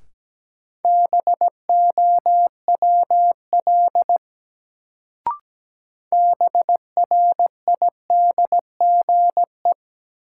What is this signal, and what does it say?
Morse code, heard as a keyed tone, about 17 wpm, 700 Hz. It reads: BOWL BRIDGE
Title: Sentences from Top 1200 Words - New Word Drill Alt - 17wpm